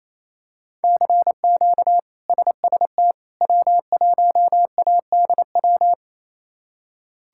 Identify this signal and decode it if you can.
Morse code, heard as a keyed tone, about 27 wpm, 700 Hz. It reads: CQ SST W1ADW
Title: K1USN SST QSO Practice - 28wpm